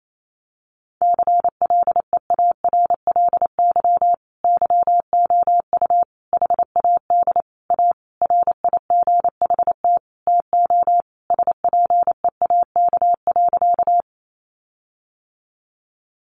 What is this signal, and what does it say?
Morse code, heard as a keyed tone, about 28 wpm, 700 Hz. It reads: CLEARLY YOU HAD A RIGHT TO SPEAK.
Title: Sentences from Top 1300 Words 28wpm